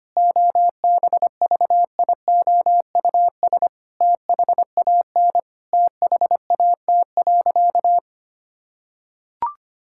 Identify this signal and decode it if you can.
Morse code, heard as a keyed tone, about 25 wpm, 700 Hz. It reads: OBVIOUS THAN THAT.
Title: Sentences from Top 1600 Words 25wpm